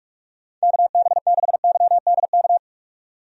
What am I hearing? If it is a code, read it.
Morse code, heard as a keyed tone, about 47 wpm, 700 Hz. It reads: KB6YDK
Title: USA Call Signs - 6 Characters - 45wpm